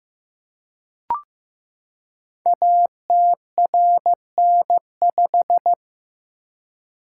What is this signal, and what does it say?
Morse code, heard as a keyed tone, about 15 wpm, 700 Hz. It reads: ATRN5
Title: Code Groups - 5 Letter:Numbers - 15wpm